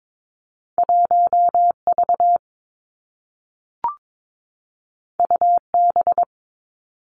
Morse code, heard as a keyed tone, about 22 wpm, 700 Hz. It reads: 1 V U B
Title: Two Challenging Letter-Number Combinations 22wpm